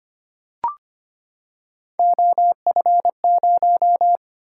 Morse code, heard as a keyed tone, about 25 wpm, 700 Hz. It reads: OF0
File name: Three Letter-Number Combinations - 25wpm